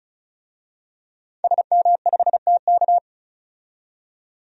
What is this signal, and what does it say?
Morse code, heard as a keyed tone, about 34 wpm, 700 Hz. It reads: SM5TK
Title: DXCC-100 Call Signs - 5 Characters - 35wpm